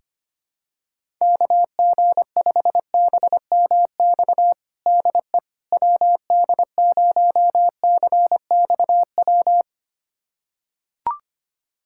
Morse code, heard as a keyed tone, about 25 wpm, 700 Hz. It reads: KG5BMX DE WD0CXW
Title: USA Callsign DE Callsign - 6 Characters - 25wpm